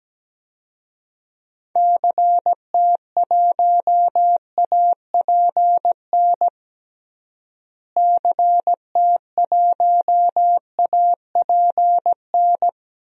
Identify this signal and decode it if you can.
Morse code, heard as a keyed tone, about 17 wpm, 700 Hz. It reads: CT1APN CT1APN
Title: DXCC-100 Call Signs - 6 Characters - Repeated 2x - 17wpm